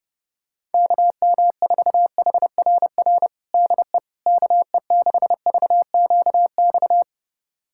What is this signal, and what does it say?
Morse code, heard as a keyed tone, about 32 wpm, 700 Hz. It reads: KM4HRR DE KE6VQX
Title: USA Callsign DE Callsign 30wpm